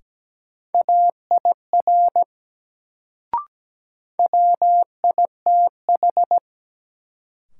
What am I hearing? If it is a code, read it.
Morse code, heard as a keyed tone, about 17 wpm, 700 Hz. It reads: AIR WITH
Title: Sets of 2 Words - New Word Drill - 17wpm